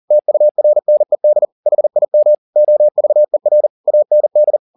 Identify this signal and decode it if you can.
Morse code, heard as a keyed tone, about 40 wpm, 600 Hz. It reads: TURNED HIM OVER AND
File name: the old man and the sea 2 40wpm